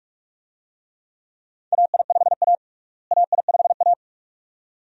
Morse code, heard as a keyed tone, about 42 wpm, 700 Hz. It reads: AI5A AI5A
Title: USA Call Signs - 4 Characters - Repeated 2x - 45wpm